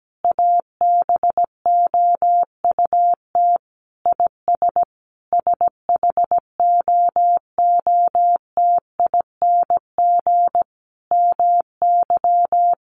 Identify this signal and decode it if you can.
Morse code, heard as a keyed tone, about 17 wpm, 700 Hz. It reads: ABOUT IS SHOOTING MY
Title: A Sound Of Thunder 17wpm